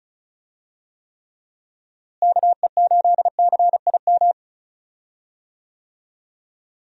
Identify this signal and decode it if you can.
Morse code, heard as a keyed tone, about 34 wpm, 700 Hz. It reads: KE8CIM
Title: USA Call Signs - 6 Characters - Repeated 3x - 35wpm